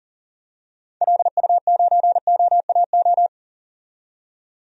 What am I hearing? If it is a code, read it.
Morse code, heard as a keyed tone, about 38 wpm, 700 Hz. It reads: LU9OAO
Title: DXCC-100 Call Signs - 6 Characters - 40wpm